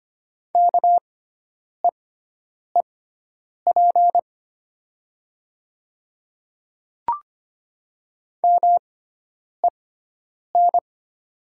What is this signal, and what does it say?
Morse code, heard as a keyed tone, about 25 wpm, spaced as for 9 wpm, 700 Hz. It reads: KEEP MEN
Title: 6x Character Spacing - Top 300 Words - 25wpm